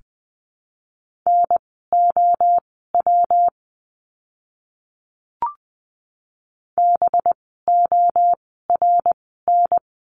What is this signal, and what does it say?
Morse code, heard as a keyed tone, about 20 wpm, spaced as for 15 wpm, 700 Hz. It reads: NOW BORN
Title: 185 Lesson 30f - Letter b - Words 20wpm